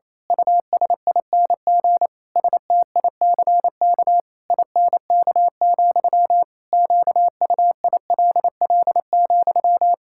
Morse code, heard as a keyed tone, about 28 wpm, 700 Hz. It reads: USING STICK INK, QUILL,
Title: Pangram Challenge v2 - 28wpm